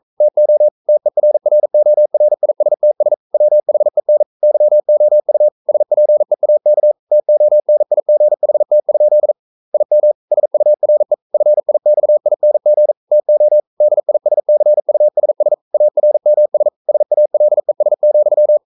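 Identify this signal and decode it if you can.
Morse code, heard as a keyed tone, about 42 wpm, 600 Hz. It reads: TO TERRORISTS WHEN YOU SPEAK TONIGHT? IM SURE FIXING TO DISCUSS ARMS SALES,
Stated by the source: Morse code practice at 41 wpm, from CW med Gnister.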